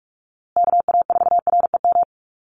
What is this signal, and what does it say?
Morse code, heard as a keyed tone, about 44 wpm, 700 Hz. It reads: KA4REM
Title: USA Call Signs Repeated 2x 45wpm